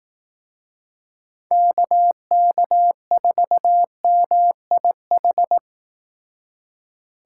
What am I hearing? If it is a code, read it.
Morse code, heard as a keyed tone, about 18 wpm, 700 Hz. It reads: KK4MIH
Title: Speed Racing - USA Call Signs 15wpm